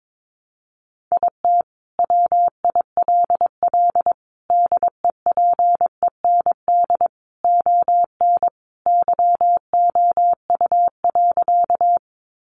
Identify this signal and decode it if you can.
Morse code, heard as a keyed tone, about 22 wpm, 700 Hz. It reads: IT WILL DEPEND ON YOU.
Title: Sentences from Top 1300 Words 22wpm